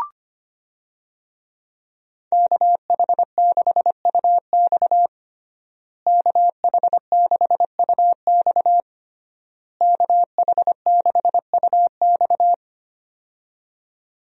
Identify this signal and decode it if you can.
Morse code, heard as a keyed tone, about 25 wpm, 700 Hz. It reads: KH6UX KH6UX KH6UX
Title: USA Call Signs - 5 Characters - Repeated 3x - 25wpm